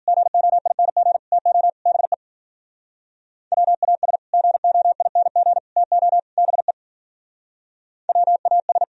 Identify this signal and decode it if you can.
Morse code, heard as a keyed tone, about 52 wpm, 700 Hz. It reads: GOING TO BE WAS GOING TO BE WAS